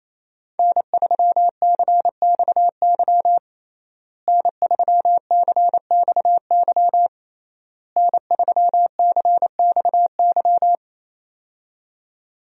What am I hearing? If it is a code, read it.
Morse code, heard as a keyed tone, about 28 wpm, 700 Hz. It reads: N3CXY N3CXY N3CXY
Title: USA Call Signs - 5 Characters - Repeated 3x - 28wpm